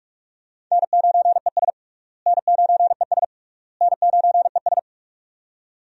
Morse code, heard as a keyed tone, about 46 wpm, 700 Hz. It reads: N9ES N9ES N9ES
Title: USA Call Signs - 4 Characters - Repeated 3x - 45wpm